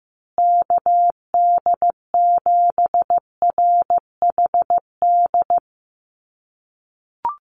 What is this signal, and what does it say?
Morse code, heard as a keyed tone, about 15 wpm, 700 Hz. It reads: KD7RHD
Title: USA Call Signs Repeated 3x 15wpm